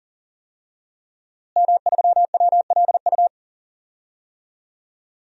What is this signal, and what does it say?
Morse code, heard as a keyed tone, about 40 wpm, 700 Hz. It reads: M3WLU
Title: DXCC-100 Call Signs - 5 Characters - Repeated 3x - 40wpm